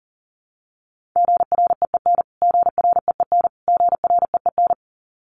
Morse code, heard as a keyed tone, about 40 wpm, 700 Hz. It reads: GREEN GREEN GREEN
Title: Top 500 Words Repeated 40wpm